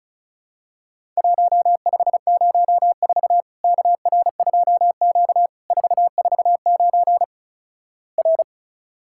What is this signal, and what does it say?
Morse code, heard as a keyed tone, about 36 wpm, 700 Hz. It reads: 1504 KR2Q 449 R
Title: Worked All Europe DX Contest - QTC Exchange - 35wpm